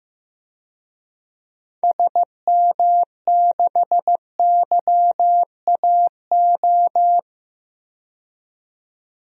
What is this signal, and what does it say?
Morse code, heard as a keyed tone, about 15 wpm, 700 Hz. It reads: SM6YAO
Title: DXCC-100 Call Signs - 6 Characters - 15wpm